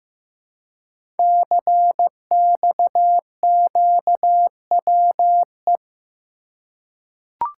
Morse code, heard as a keyed tone, about 15 wpm, 700 Hz. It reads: CXQWE
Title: Code Groups - 5 Letters - 15wpm